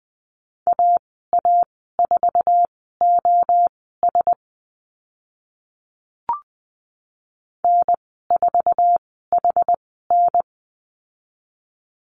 Morse code, heard as a keyed tone, about 20 wpm, spaced as for 15 wpm, 700 Hz. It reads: AA4OS N4HN
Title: 076 Lesson 13g - Letter L - Call Signs 20wpm